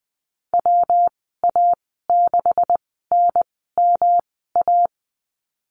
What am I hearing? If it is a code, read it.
Morse code, heard as a keyed tone, about 20 wpm, spaced as for 15 wpm, 700 Hz. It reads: WA6NMA
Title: 154 Lesson 25g - Letter p - Call Signs 20wpm